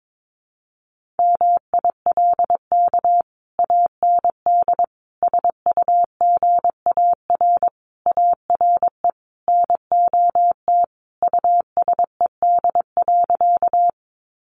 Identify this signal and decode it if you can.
Morse code, heard as a keyed tone, about 22 wpm, 700 Hz. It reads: MILK AND SUGAR ARE NOT USED.
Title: Sentences from Top 1500 Words 22wpm